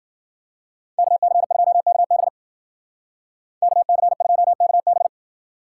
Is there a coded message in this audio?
Yes: Morse code from 1.0 to 5.1 s, heard as a keyed tone, about 53 wpm, 700 Hz. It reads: KC2KB KC2KB